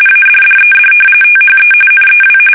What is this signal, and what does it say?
RTTY, about 45.45 baud. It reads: ADDED THE NEW
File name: fsk850